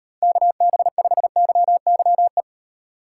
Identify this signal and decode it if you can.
Morse code, heard as a keyed tone, about 38 wpm, 700 Hz. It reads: KB5YYE